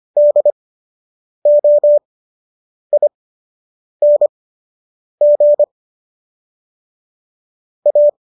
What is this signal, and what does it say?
Morse code, heard as a keyed tone, about 25 wpm, spaced as for 8 wpm, 590 Hz. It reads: DOING A